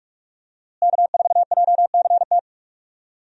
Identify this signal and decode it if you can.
Morse code, heard as a keyed tone, about 43 wpm, 700 Hz. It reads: K4JCT
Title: USA Call Signs - 5 Characters - Repeated 3x - 45wpm